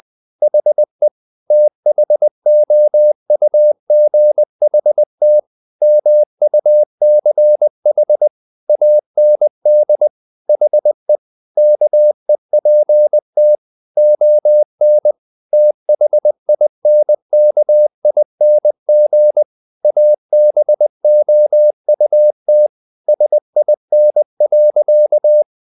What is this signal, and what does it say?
Morse code, heard as a keyed tone, about 20 wpm, 600 Hz. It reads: HE THOUGHT MUCH AND HE KEPT ON THINKING ABOUT SIN.
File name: the old man and the sea part 2 20wpm